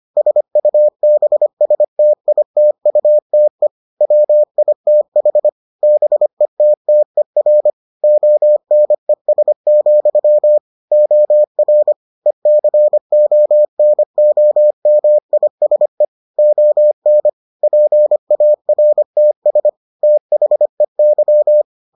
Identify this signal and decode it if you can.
Morse code, heard as a keyed tone, about 25 wpm, 600 Hz. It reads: SUBSTITUTE WITH BETTER ONES, OR ECONOMISE ON PARTS THEY